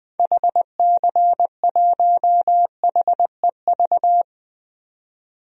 Morse code, heard as a keyed tone, about 20 wpm, 700 Hz. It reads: HC1HEV
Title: DXCC-100 Call Signs - 6 Characters - 20wpm